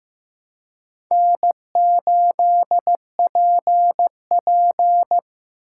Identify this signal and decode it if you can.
Morse code, heard as a keyed tone, about 15 wpm, 700 Hz. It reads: N8PP
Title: USA Call Signs - 3x Character Spacing - 15wpm